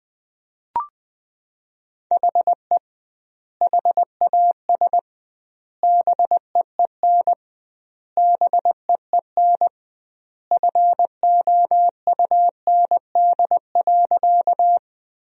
Morse code, heard as a keyed tone, about 21 wpm, 700 Hz. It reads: HE HAS BEEN BEEN FOUND.
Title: 2x Word-Spacing - Sentences from Top 300 Words - 20wpm